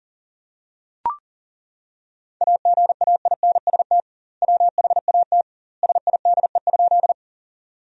Morse code, heard as a keyed tone, about 40 wpm, 700 Hz. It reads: AGAINST WHAT SIDE?
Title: 2x Word-Spacing - Sentences from Top 400 Words - 40wpm